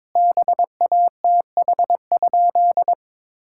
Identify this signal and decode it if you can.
Morse code, heard as a keyed tone, about 22 wpm, 700 Hz. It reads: BATH?
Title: Sentences from Top 1600 Words 22wpm